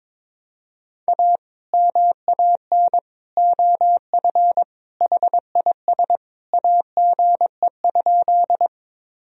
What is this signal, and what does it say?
Morse code, heard as a keyed tone, about 22 wpm, 700 Hz. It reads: A MAN OF HIS AGE?
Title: Sentences from Top 1000 Words - Review Part 3 - 22wpm